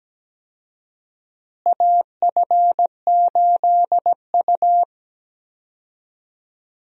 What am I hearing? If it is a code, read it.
Morse code, heard as a keyed tone, about 17 wpm, 700 Hz. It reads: AF8U
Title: USA Call Signs - 4 Characters - Repeated 2x - 17wpm